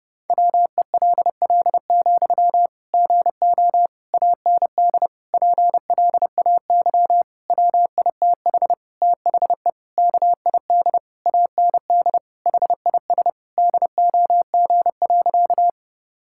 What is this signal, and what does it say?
Morse code, heard as a keyed tone, about 31 wpm, 700 Hz. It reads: WELL, GO AND PLAY WITH THE KID AND HIS DOG.
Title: Sentences from Top 1600 Words 30wpm